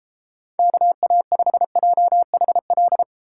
Morse code, heard as a keyed tone, about 33 wpm, 700 Hz. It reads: KA5JHL